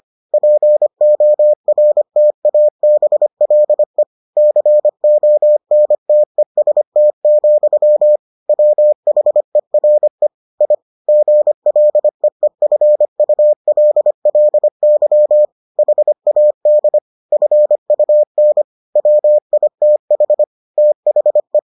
Morse code, heard as a keyed tone, about 25 wpm, 600 Hz. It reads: PORTABLE CONTEST, WHERE I GLEEFULLY HAD FUN WITH THE